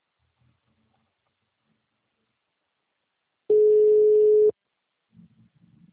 Ringback tone (Europe, 425 Hz).